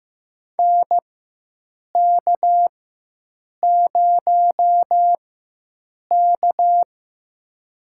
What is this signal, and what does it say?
Morse code, heard as a keyed tone, about 15 wpm, spaced as for 7 wpm, 700 Hz. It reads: NK0K
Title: USA Call Signs - 4 Characters - 4x Character Spacing - 15wpm